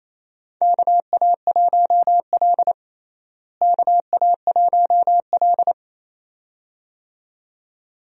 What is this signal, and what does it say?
Morse code, heard as a keyed tone, about 27 wpm, 700 Hz. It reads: KA1L KA1L
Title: USA Call Signs - 4 Characters - Repeated 2x - 28wpm